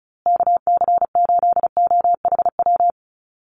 Morse code, heard as a keyed tone, about 34 wpm, 700 Hz. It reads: KC8OHW
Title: USA Call Signs Repeated 3x 35wpm